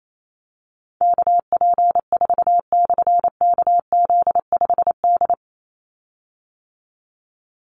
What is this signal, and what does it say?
Morse code, heard as a keyed tone, about 28 wpm, 700 Hz. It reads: KP4/KZ5D
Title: Call Signs - 28wpm